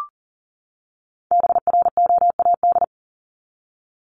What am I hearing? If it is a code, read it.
Morse code, heard as a keyed tone, about 37 wpm, 700 Hz. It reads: BROAD
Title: Top 751-1000 Words 40wpm